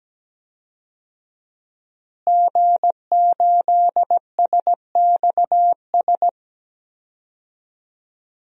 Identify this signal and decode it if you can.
Morse code, heard as a keyed tone, about 17 wpm, 700 Hz. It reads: G8SXS
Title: DXCC-100 Call Signs - 5 Characters - Repeated 3x - 17wpm